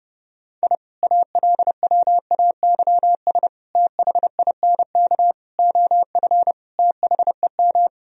Morse code, heard as a keyed tone, about 30 wpm, 700 Hz. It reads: I ALWAYS THINK OF THEM
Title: Sentences from Top 1000 Words - Review Part 3 - 30wpm